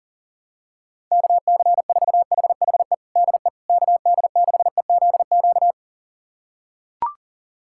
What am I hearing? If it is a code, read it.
Morse code, heard as a keyed tone, about 40 wpm, 700 Hz. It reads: KC4HHE DE KD6EZQ
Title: USA Callsign DE Callsign - 6 Characters - 40wpm